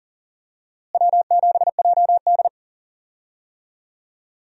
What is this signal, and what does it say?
Morse code, heard as a keyed tone, about 37 wpm, 700 Hz. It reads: W7JD
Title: USA Call Signs - 4 Characters - Repeated 3x - 40wpm